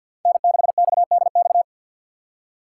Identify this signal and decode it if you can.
Morse code, heard as a keyed tone, about 52 wpm, 690 Hz. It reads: N6XDX
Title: USA Call Signs - 5 Characters - 2x Character Spacing - 50wpm